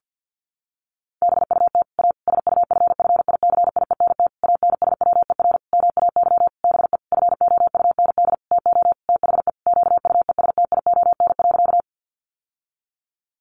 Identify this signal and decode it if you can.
Morse code, heard as a keyed tone, about 50 wpm, 700 Hz. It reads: BUT A SUFFICIENT ANSWER MAY BE FOUND TO THE QUESTION.
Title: Sentences from Top 1400 Words 50wpm